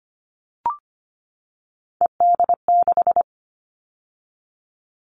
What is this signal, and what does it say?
Morse code, heard as a keyed tone, about 25 wpm, 700 Hz. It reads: ED6
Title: Callsign Prefixes for Top 50 DXCC - 25wpm